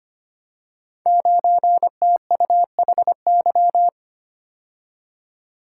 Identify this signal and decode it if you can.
Morse code, heard as a keyed tone, about 25 wpm, 700 Hz. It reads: 9TUHY
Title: Code Groups - 5 Letter:Numbers - 25wpm